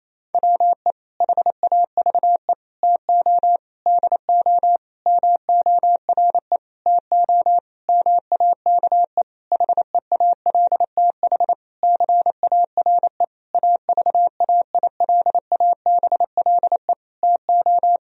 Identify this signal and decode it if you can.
Morse code, heard as a keyed tone, about 28 wpm, 700 Hz. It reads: WE HAVE TO DO MORE TO MAKE HEALTH CARE AVAILABLE TO
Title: Sentences from Top 1000 Words - Review Part 3 - 28wpm